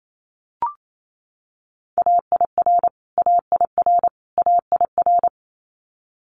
Morse code, heard as a keyed tone, about 28 wpm, 700 Hz. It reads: AIR AIR AIR
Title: Multi-letter Phonemes Repeated 3x 28wpm